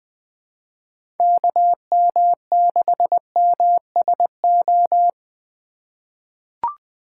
Morse code, heard as a keyed tone, about 21 wpm, 700 Hz. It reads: KM6MSO
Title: USA Call Signs - 6 Characters - 4x Character Spacing - 20wpm